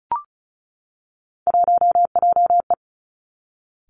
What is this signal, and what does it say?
Morse code, heard as a keyed tone, about 36 wpm, 700 Hz. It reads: 1JE
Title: Three Challenging Letter-Number Combinations 35wpm